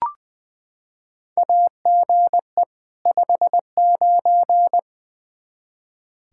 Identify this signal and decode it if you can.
Morse code, heard as a keyed tone, about 20 wpm, 700 Hz. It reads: AGE 59
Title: Practice QSO - Exchange 3 - More Info - 20wpm